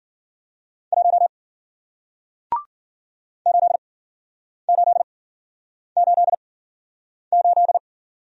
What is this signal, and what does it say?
Morse code, heard as a keyed tone, about 52 wpm, spaced as for 9 wpm, 700 Hz. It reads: 1 8888